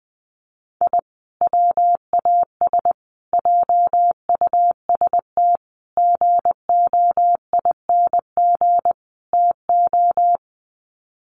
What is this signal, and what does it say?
Morse code, heard as a keyed tone, about 20 wpm, 700 Hz. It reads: I WAS JUST GOING TO
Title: Sets of 5 Words Encore 20wpm